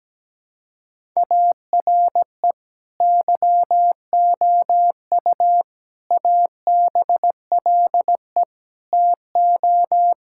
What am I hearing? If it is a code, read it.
Morse code, heard as a keyed tone, about 17 wpm, 700 Hz. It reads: ARE YOU ABLE TO
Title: Sentences from Top 1000 Words - Review Part 5 - 17wpm